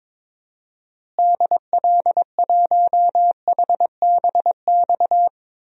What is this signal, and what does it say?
Morse code, heard as a keyed tone, about 22 wpm, 700 Hz. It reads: DL1HBX
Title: DXCC-100 Call Signs - 6 Characters - Repeated 2x - 22wpm